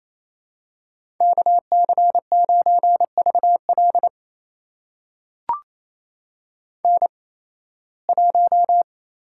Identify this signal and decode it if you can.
Morse code, heard as a keyed tone, about 28 wpm, spaced as for 18 wpm, 700 Hz. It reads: KC9VL N1
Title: USA Call Signs - 5 Characters - 8x Character Spacing - 28wpm